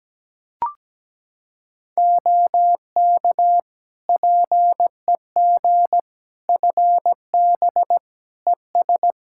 Morse code, heard as a keyed tone, about 17 wpm, 700 Hz. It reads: OK PEG FB ES
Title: Practice QSO - Exchange 2 - Info - 17wpm